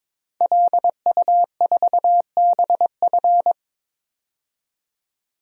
Morse code, heard as a keyed tone, about 22 wpm, 700 Hz. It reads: LU4BF
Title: DXCC-100 Call Signs - 5 Characters - 22wpm